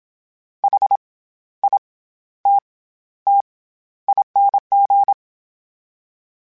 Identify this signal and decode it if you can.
Morse code, heard as a keyed tone, about 27 wpm, 800 Hz. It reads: H I T T ING
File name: Special INGs-mp3s / Special INGs-mp3s / SlowToFaster ing @25